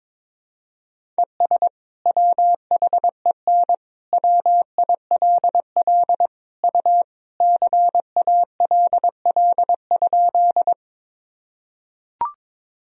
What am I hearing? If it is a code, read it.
Morse code, heard as a keyed tone, about 22 wpm, 700 Hz. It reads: ES WHEN WILL U CALL?
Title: Sentences from Top 100 Words and Abbreviations 22wpm